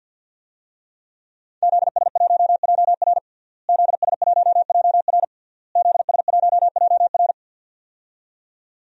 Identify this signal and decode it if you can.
Morse code, heard as a keyed tone, about 52 wpm, 700 Hz. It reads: ZS1JR ZS1JR ZS1JR
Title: DXCC-100 Call Signs - 5 Characters - Repeated 3x - 50wpm